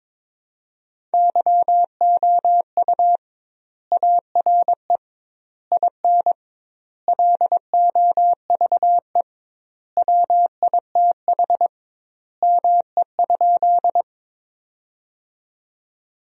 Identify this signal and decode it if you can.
Morse code, heard as a keyed tone, about 21 wpm, 700 Hz. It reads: YOU ARE IN LOVE WITH ME?
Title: 2x Word-Spacing - Sentences from Top 400 Words - 22wpm